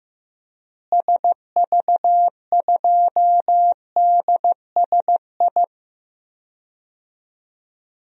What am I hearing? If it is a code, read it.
Morse code, heard as a keyed tone, about 15 wpm, 700 Hz. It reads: SV2DSI
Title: DXCC-100 Call Signs - 6 Characters - Repeated 3x - 15wpm